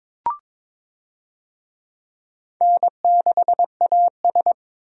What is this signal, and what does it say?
Morse code, heard as a keyed tone, about 22 wpm, 700 Hz. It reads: N6AS
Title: USA Call Signs - 4 Characters - 22wpm